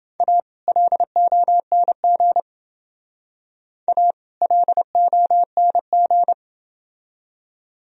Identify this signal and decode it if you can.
Morse code, heard as a keyed tone, about 28 wpm, 700 Hz. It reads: A LONG A LONG